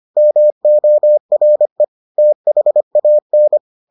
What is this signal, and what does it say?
Morse code, heard as a keyed tone, about 25 wpm, 600 Hz. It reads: MORE THAN